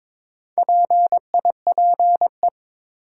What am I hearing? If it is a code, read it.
Morse code, heard as a keyed tone, about 21 wpm, 700 Hz. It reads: PIPE
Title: Sentences from Top 1500 Words - New Word Drill - 22wpm